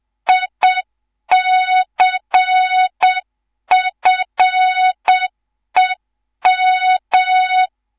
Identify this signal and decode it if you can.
Morse code, heard as a keyed tone, about 7 wpm, 3700 Hz. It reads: ICFEM
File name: Morze-HA2MN / hang / 11 GYAKORLAT